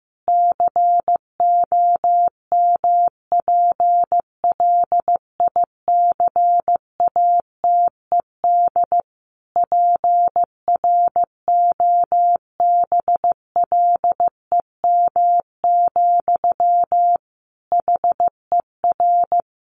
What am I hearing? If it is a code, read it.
Morse code, heard as a keyed tone, about 15 wpm, 700 Hz. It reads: COMPLICATED PROBLEM, HER
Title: Sentences from Top 1200 Words 15wpm